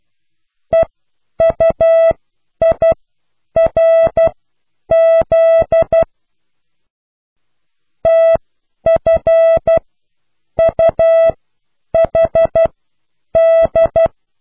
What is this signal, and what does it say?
Morse code, heard as a keyed tone, about 12 wpm, spaced as for 8 wpm, 660 Hz. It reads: EUIRZ TFUHD